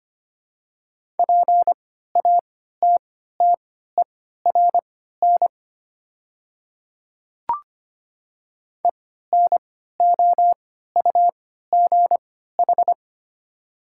Morse code, heard as a keyed tone, about 25 wpm, spaced as for 14 wpm, 700 Hz. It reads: PATTERN ENOUGH
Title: 3x Character Spacing - Top 500 Words - 25wpm